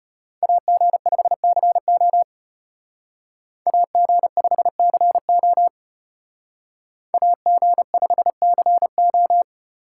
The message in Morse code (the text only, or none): AG5CO AG5CO AG5CO